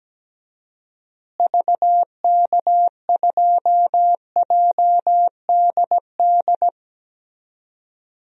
Morse code, heard as a keyed tone, about 17 wpm, 700 Hz. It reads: VK2JDD
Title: DXCC-100 Call Signs - 6 Characters - Repeated 2x - 17wpm